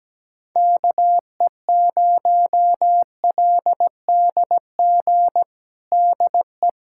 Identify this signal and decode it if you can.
Morse code, heard as a keyed tone, about 17 wpm, 700 Hz. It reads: KE0LDG DE
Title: USA Callsign DE Callsign - 6 Characters - 17wpm